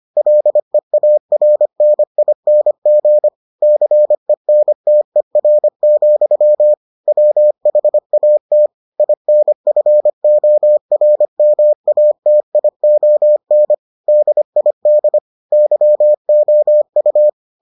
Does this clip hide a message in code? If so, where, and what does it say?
Yes: Morse code from 0.2 to 17.3 s, heard as a keyed tone, about 25 wpm, 600 Hz. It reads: LEARNING CENTER, WHAT INFORMATION DID YOU